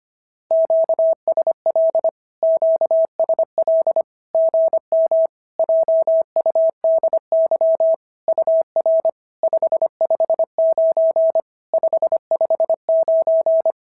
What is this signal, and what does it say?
Morse code, heard as a keyed tone, about 25 wpm, 650 Hz. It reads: QSL QSL GM JUDY UR 559 559
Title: SOTA QSO Practice 25wpm